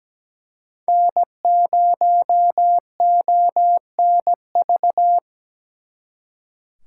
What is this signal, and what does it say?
Morse code, heard as a keyed tone, about 20 wpm, 700 Hz. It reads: N0ONV
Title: Speed Racing - USA 1x3 Call Signs 17wpm